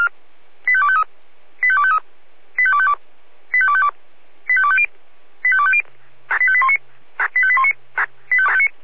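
Five-tone selective call (ZVEI), heard as quick runs of five tones, tones of about 70 ms. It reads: 85253 85253 85252 85252 85250 85250 85819 85819
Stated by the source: ZVEI five-tone selective call received on SDR.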